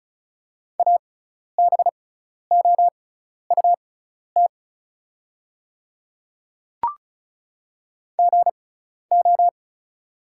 Morse code, heard as a keyed tone, about 36 wpm, spaced as for 12 wpm, 700 Hz. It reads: ABOUT GO
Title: 6x Character Spacing - Top 100 Words - 35wpm